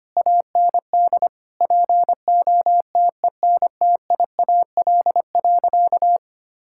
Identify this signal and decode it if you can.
Morse code, heard as a keyed tone, about 25 wpm, 700 Hz. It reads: AND POTENTIAL.
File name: practice / 25wpm / Sentences with 9-Word Phrases - 25wpm